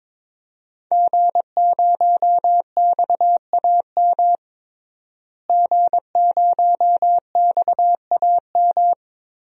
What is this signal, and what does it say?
Morse code, heard as a keyed tone, about 22 wpm, 700 Hz. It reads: G0XAM G0XAM
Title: DXCC-100 Call Signs - 5 Characters - Repeated 3x - 22wpm